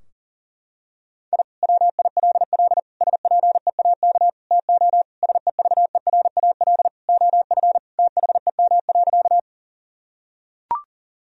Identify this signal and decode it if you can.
Morse code, heard as a keyed tone, about 40 wpm, 700 Hz. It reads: I WILL SPEAK TO SEVERAL OF THEM.
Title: Sentences from Top 1000 Words - Review Part 2 - 40wpm